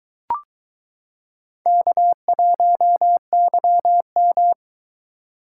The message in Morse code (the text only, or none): K1YM